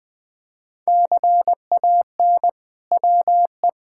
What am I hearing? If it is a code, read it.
Morse code, heard as a keyed tone, about 20 wpm, 700 Hz. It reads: CAN WE
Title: Sentences from Top 1000 Words - Review Part 5 - 20wpm